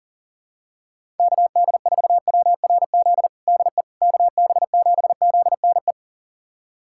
Morse code, heard as a keyed tone, about 40 wpm, 700 Hz. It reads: KD4WRZ DE KB7ZNE
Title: USA Callsign DE Callsign - 6 Characters - 40wpm